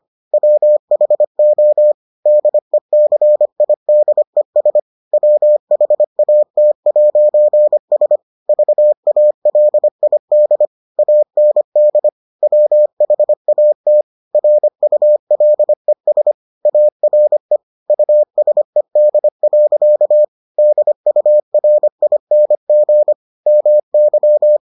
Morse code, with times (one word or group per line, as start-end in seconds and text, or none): 0.3-1.9 WHO
2.3-4.8 DECIDES
5.1-8.2 WHAT'S
8.5-10.7 VALID
11.0-12.1 AND
12.4-14.0 WHAT
14.3-16.3 RULES
16.7-17.6 ARE
17.9-20.2 USED.
20.6-23.1 DURING
23.5-24.6 MY